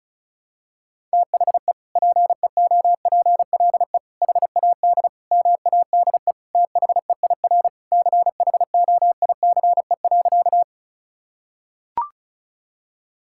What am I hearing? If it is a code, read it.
Morse code, heard as a keyed tone, about 34 wpm, 700 Hz. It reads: THE PEOPLE HAD MADE THEIR CHOICE.
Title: Sentences from Top 1000 Words - Review Part 6 - 35wpm